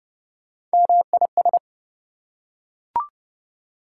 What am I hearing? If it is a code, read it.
Morse code, heard as a keyed tone, about 31 wpm, 700 Hz. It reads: MIS
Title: Prefixes and Suffixes 30wpm